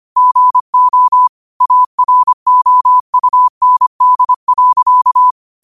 Morse code, heard as a keyed tone, about 25 wpm, 1000 Hz. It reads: GO AROUND.